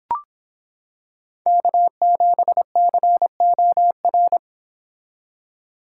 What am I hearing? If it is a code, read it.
Morse code, heard as a keyed tone, about 26 wpm, 700 Hz. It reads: K7COR